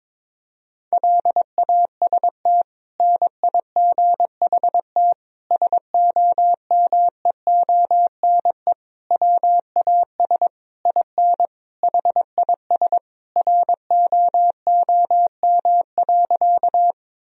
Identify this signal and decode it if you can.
Morse code, heard as a keyed tone, about 22 wpm, 700 Hz. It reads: LAST NIGHT SOMEONE WAS IN HIS ROOM.
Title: Sentences from Top 1000 Words - Review Part 4 - 22wpm